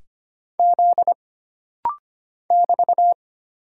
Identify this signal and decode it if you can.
Morse code, heard as a keyed tone, about 25 wpm, 700 Hz. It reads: Z <BT>
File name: Single Character (Letter:Number:Punctuation:Prosign) - Rapid-Fire - 25wpm